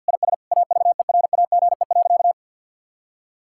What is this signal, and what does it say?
Morse code, heard as a keyed tone, about 46 wpm, 700 Hz. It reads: IS AVERAGE.
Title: Sentences from Top 1000 Words - Review Part 6 - 50wpm